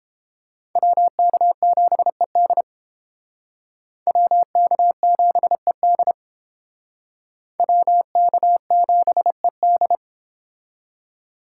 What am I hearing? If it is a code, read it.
Morse code, heard as a keyed tone, about 32 wpm, 700 Hz. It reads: WK7ED WK7ED WK7ED